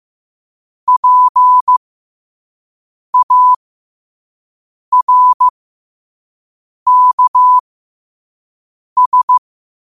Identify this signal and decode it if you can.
Morse code, heard as a keyed tone, about 15 wpm, spaced as for 5 wpm, 1000 Hz. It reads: PARKS